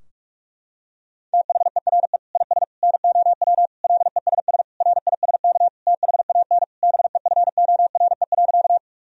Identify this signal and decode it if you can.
Morse code, heard as a keyed tone, about 46 wpm, 700 Hz. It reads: THERE IS NOW LESS RISK THAN BEFORE.
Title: Sentences from Top 1000 Words - Review Part 4 - 45wpm